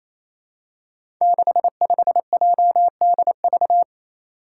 Morse code, heard as a keyed tone, about 28 wpm, 700 Hz. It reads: 65JDV
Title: Five Challenging Letter-Number Combinations 28wpm